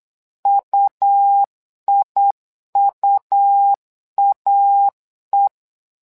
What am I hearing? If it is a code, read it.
Morse code, heard as a keyed tone, about 8 wpm, 800 Hz. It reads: UIUAE
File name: LEZIONI / 2 UHDN / 2 U 8wpm 2028